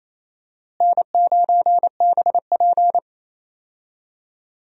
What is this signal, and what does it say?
Morse code, heard as a keyed tone, about 28 wpm, 700 Hz. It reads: N9BP
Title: USA Call Signs - 4 Characters - Repeated 2x - 28wpm